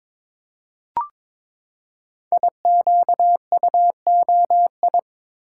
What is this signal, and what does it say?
Morse code, heard as a keyed tone, about 22 wpm, 700 Hz. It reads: IQUOI
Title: Code Groups - 5 Letters - 22wpm